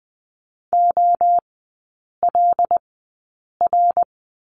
Morse code, heard as a keyed tone, about 20 wpm, spaced as for 8 wpm, 700 Hz. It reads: OLR
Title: 074 Lesson 13e - Letter L - Three Characters 20wpm